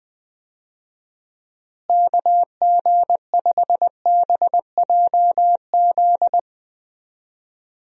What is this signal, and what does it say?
Morse code, heard as a keyed tone, about 20 wpm, 700 Hz. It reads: KG5BJZ
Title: USA Call Signs - 6 Characters - Repeated 2x - 20wpm